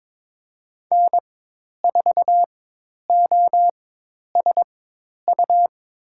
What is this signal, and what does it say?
Morse code, heard as a keyed tone, about 22 wpm, spaced as for 10 wpm, 700 Hz. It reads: N4OSU
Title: USA Call Signs - 5 Characters - 4x Character Spacing - 22wpm